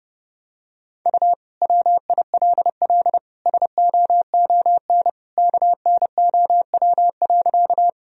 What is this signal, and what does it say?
Morse code, heard as a keyed tone, about 30 wpm, 700 Hz. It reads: U WILL SOON KNOW.
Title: Sentences from Top 400 Words and Abbreviations 30wpm